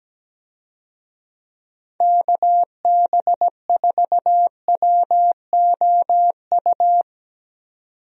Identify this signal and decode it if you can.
Morse code, heard as a keyed tone, about 17 wpm, 700 Hz. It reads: KB4WOU
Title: USA Call Signs - 6 Characters - 17wpm